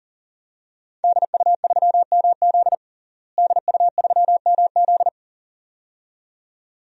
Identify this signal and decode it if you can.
Morse code, heard as a keyed tone, about 40 wpm, 700 Hz. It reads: DU3MZ DU3MZ
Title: DXCC-100 Call Signs - 5 Characters - Repeated 2x - 40wpm